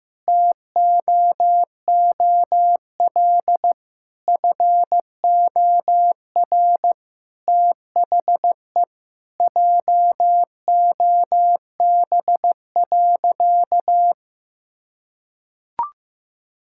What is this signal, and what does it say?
Morse code, heard as a keyed tone, about 16 wpm, 700 Hz. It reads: TOOL FOR THE JOB.
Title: Sentences from Top 1000 Words - Review Part 6 - 15wpm